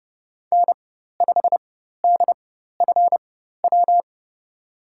Morse code, heard as a keyed tone, about 30 wpm, spaced as for 14 wpm, 700 Hz. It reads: N5DFW
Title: USA Call Signs - 4x Character Spacing - 30wpm